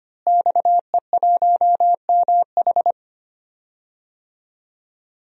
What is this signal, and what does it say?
Morse code, heard as a keyed tone, about 25 wpm, 700 Hz. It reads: XE1MH
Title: DXCC-100 Call Signs - 5 Characters - Repeated 3x - 25wpm